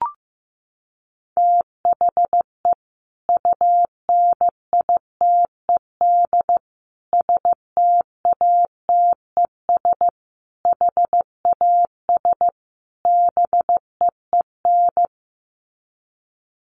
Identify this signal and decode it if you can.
Morse code, heard as a keyed tone, about 15 wpm, 700 Hz. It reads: THE UNITED STATES HAS BEEN
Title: Sets of 5 Words Encore2 15wpm